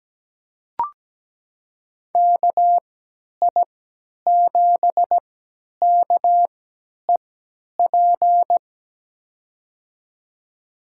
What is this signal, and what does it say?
Morse code, heard as a keyed tone, about 17 wpm, spaced as for 10 wpm, 700 Hz. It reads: KI7KEP
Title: USA Call Signs - 6 Characters - 3x Character Spacing - 17wpm